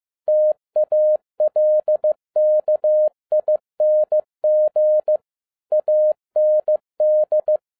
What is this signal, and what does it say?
Morse code, heard as a keyed tone, about 15 wpm, 610 Hz. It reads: TALKING AND
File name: 2011-07-mp3 / n47 JOHN K3TN 15wpm